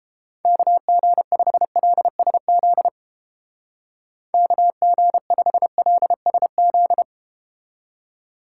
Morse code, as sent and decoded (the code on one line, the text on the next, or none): -.- --. ..... .-.. ... --.. / -.- --. ..... .-.. ... --..
KG5LSZ KG5LSZ